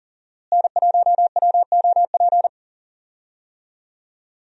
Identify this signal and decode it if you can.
Morse code, heard as a keyed tone, about 37 wpm, 700 Hz. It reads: N1WOP
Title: USA Call Signs - 5 Characters - Repeated 3x - 40wpm